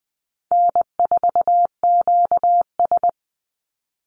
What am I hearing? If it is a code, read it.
Morse code, heard as a keyed tone, about 20 wpm, 700 Hz. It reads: N4QS
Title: Easy Call Signs - 20wpm